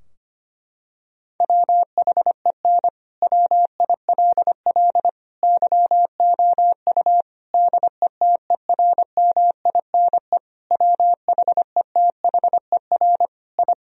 Morse code, heard as a keyed tone, about 25 wpm, 700 Hz. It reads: WHEN WILL YOU DETERMINE WHETHER I
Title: Sentences from Top 1000 Words - Review Part 4 - 25wpm